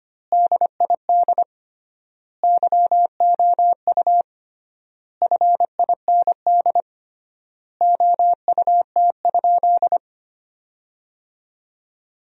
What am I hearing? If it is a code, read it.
Morse code, heard as a keyed tone, about 25 wpm, 700 Hz. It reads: DID YOU FIND OUT?
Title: 3x Word-Spacing - Sentences from Top 200 Words - 25wpm